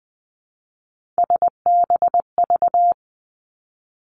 Morse code, heard as a keyed tone, about 20 wpm, 700 Hz. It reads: SBV
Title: Three Challenging Letter-Number Combinations 20wpm